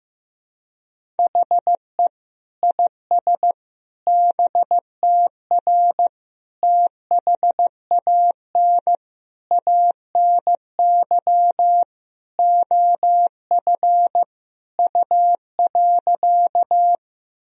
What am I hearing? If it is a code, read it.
Morse code, heard as a keyed tone, about 16 wpm, 700 Hz. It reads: HE IS BTR THAN ANY OF U.
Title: Sentences from Top 400 Words and Abbreviations 15wpm